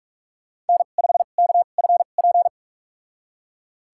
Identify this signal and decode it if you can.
Morse code, heard as a keyed tone, about 41 wpm, spaced as for 31 wpm, 700 Hz. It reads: N5KFP
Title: USA Call Signs - 5 Characters - 2x Character Spacing - 45wpm